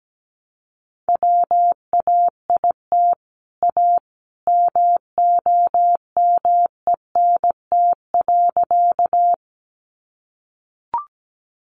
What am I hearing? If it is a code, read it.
Morse code, heard as a keyed tone, about 17 wpm, 700 Hz. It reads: WAIT A MOMENT.
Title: Sentences from Top 1000 Words 17wpm